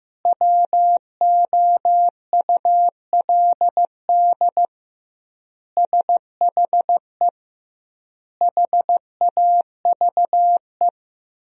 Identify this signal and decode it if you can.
Morse code, heard as a keyed tone, about 15 wpm, 700 Hz. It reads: WOULD SHE HAVE
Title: 2x Word-Spacing - Sentences from Top 100 Words - 15wpm